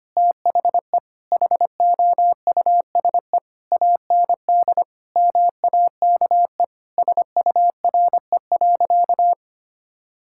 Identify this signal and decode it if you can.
Morse code, heard as a keyed tone, about 25 wpm, 700 Hz. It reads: THE HOUSE AND MAKE SURE.
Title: Sentences from Top 1000 Words - Review Part 5 - 25wpm